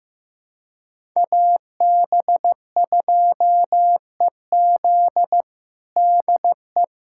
Morse code, heard as a keyed tone, about 15 wpm, 700 Hz. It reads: AB2EZ DE
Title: USA Callsign DE Callsign - 5 Characters - 15wpm